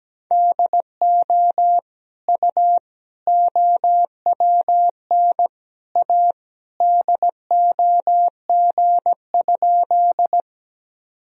Morse code, heard as a keyed tone, about 17 wpm, 700 Hz. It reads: DO U OWN A DOG?
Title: Sentences from Top 400 Words and Abbreviations 17wpm